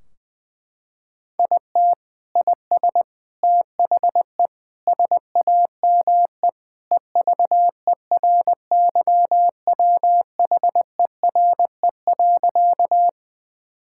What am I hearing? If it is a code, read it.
Morse code, heard as a keyed tone, about 19 wpm, 700 Hz. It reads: IT IS THE SAME EVERYWHERE.
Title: Sentences from Top 1600 Words 20wpm